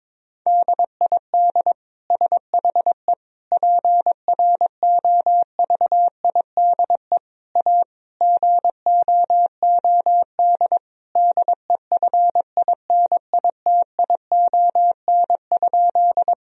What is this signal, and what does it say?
Morse code, heard as a keyed tone, about 22 wpm, 700 Hz. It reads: DID SHE PROVIDE A GOOD DEFINITION?
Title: Sentences from Top 1000 Words - Review Part 4 - 22wpm